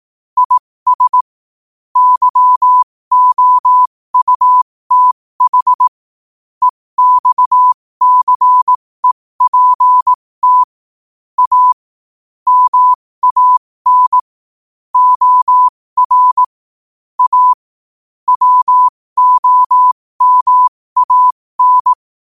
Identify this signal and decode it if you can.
Morse code, heard as a keyed tone, about 18 wpm, spaced as for 15 wpm, 1000 Hz. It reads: IS YOUTH EXCEPT A MAN OR A WOMAN
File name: qotdfiles / 15wpm